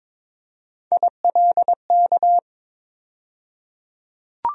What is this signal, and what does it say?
Morse code, heard as a keyed tone, about 22 wpm, 700 Hz. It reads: ILK
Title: USA Call Sign Postfixes - 3 Letters 22wpm